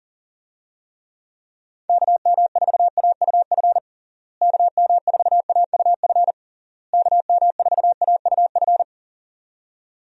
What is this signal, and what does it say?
Morse code, heard as a keyed tone, about 40 wpm, 700 Hz. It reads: KM4AUF KM4AUF KM4AUF
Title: USA Call Signs - 6 Characters - Repeated 3x - 40wpm